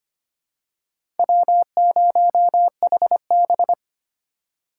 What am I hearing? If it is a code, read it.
Morse code, heard as a keyed tone, about 25 wpm, 700 Hz. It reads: W0HB
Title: USA Call Signs - 4 Characters - 4x Character Spacing - 25wpm